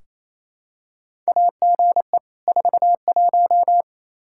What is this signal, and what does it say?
Morse code, heard as a keyed tone, about 28 wpm, 700 Hz. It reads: AGE 41
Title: Practice QSO - Exchange 3 - More Info - 28wpm